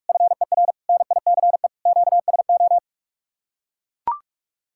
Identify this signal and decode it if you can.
Morse code, heard as a keyed tone, about 44 wpm, 700 Hz. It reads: FER NICE QSO